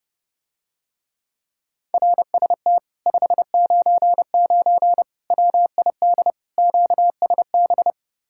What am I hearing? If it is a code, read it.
Morse code, heard as a keyed tone, about 31 wpm, 700 Hz. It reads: RST 599 WID QSB